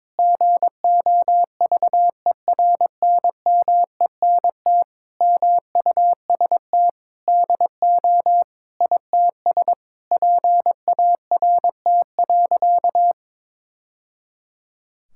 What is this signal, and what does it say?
Morse code, heard as a keyed tone, about 22 wpm, 700 Hz. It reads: GOVERNMENT MUST DO ITS PART.
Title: Sentences from Top 1000 Words - Review Part 5 - 22wpm